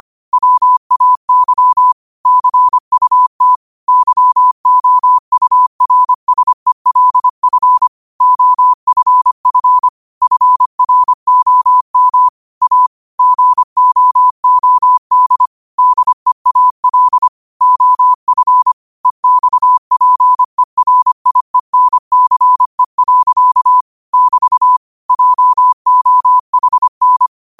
Morse code, heard as a keyed tone, about 25 wpm, 1000 Hz. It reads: WAY CUT YOURSELF OFF FROM A GOOD DEAL OF EXPERIENCE. <BT> JOHN
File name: qotdfiles / 25wpm